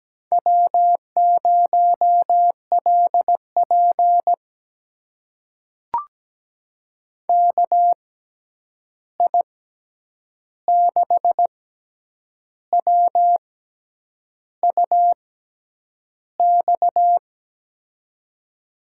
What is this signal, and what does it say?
Morse code, heard as a keyed tone, about 17 wpm, spaced as for 8 wpm, 700 Hz. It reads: W0LP KI6WUX
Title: USA Call Signs - 6x Character Spacing - 17wpm